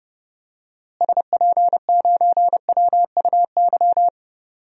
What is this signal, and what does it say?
Morse code, heard as a keyed tone, about 30 wpm, 700 Hz. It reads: SP9WUY
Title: DXCC-100 Call Signs - 6 Characters - Repeated 2x - 30wpm